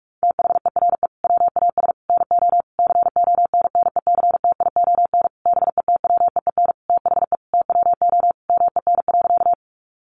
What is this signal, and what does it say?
Morse code, heard as a keyed tone, about 44 wpm, 700 Hz. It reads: THERE WAS NO CONNECTION BETWEEN THE TWO MEN.
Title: Sentences from Top 1300 Words 45wpm